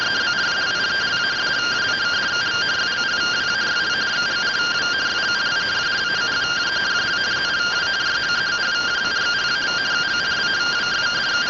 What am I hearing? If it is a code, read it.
Two-tone teleprinter signal, about 170 Hz shift.
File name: findings / CT SAGITAL 27.gen-28 rtty rx